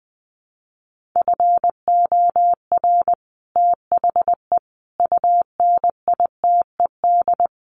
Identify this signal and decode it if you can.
Morse code, heard as a keyed tone, about 20 wpm, 700 Hz. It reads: FOR THE UNITED
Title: Sets of 5 Words Encore 20wpm